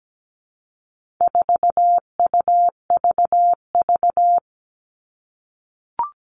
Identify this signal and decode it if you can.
Morse code, heard as a keyed tone, about 17 wpm, 700 Hz. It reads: 4UVV
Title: All Cut Numbers - 4-Digit 17wpm